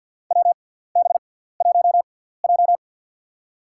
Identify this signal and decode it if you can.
Morse code, heard as a keyed tone, about 53 wpm, spaced as for 17 wpm, 700 Hz. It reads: WB1J
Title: USA Call Signs - 4 Characters - 6x Character Spacing - 50wpm